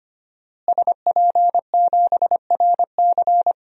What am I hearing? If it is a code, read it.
Morse code, heard as a keyed tone, about 25 wpm, 700 Hz. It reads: SP7RC
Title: DXCC-100 Call Signs - 5 Characters - Repeated 2x - 25wpm